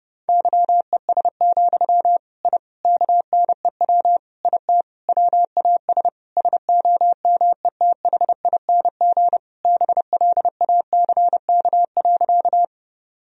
Morse code, heard as a keyed tone, about 30 wpm, 700 Hz. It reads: YES, I KNEW IT WAS SOMETHING BLACK.
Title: Sentences from Top 1000 Words - Review Part 2 - 30wpm